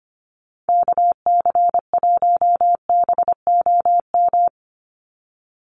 Morse code, heard as a keyed tone, about 25 wpm, 700 Hz. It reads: KC1BOM
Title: USA Call Signs 25wpm